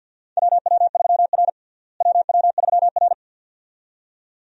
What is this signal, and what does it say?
Morse code, heard as a keyed tone, about 49 wpm, 700 Hz. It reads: WW3R WW3R
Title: USA Call Signs - 4 Characters - Repeated 2x - 50wpm